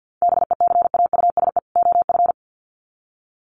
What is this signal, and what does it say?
Morse code, heard as a keyed tone, about 48 wpm, 700 Hz. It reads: BECAUSE OF
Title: Sets of 2 Words 50wpm